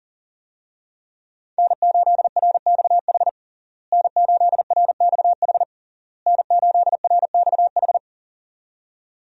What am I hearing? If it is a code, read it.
Morse code, heard as a keyed tone, about 40 wpm, 700 Hz. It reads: N8RXH N8RXH N8RXH
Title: USA Call Signs - 5 Characters - Repeated 3x - 40wpm